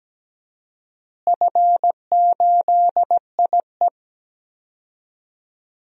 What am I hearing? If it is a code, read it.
Morse code, heard as a keyed tone, about 17 wpm, 700 Hz. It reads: F8IE
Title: DXCC-100 Call Signs - 4 Characters - Repeated 2x - 17wpm